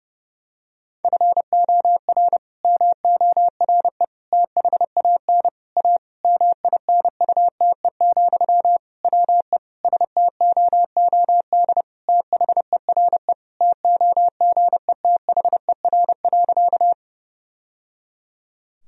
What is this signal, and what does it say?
Morse code, heard as a keyed tone, about 32 wpm, 700 Hz. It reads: FOR MORE THAN A MINUTE, WE STOOD THERE TOGETHER.
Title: Sentences from Top 1000 Words - Review Part 2 - 30wpm